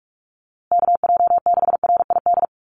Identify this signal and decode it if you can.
Morse code, heard as a keyed tone, about 44 wpm, 700 Hz. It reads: KJ6RID
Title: USA Call Signs Repeated 3x 45wpm